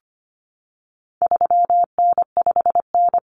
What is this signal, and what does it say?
Morse code, heard as a keyed tone, about 25 wpm, 700 Hz. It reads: 3N5N
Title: Common Cut Numbers - 4-Digit 25wpm